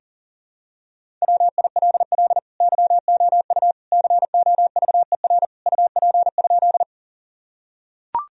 Morse code, heard as a keyed tone, about 40 wpm, 700 Hz. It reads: WILL YOU COVER UP?
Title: Sentences from Top 1000 Words - Review Part 1 - 40wpm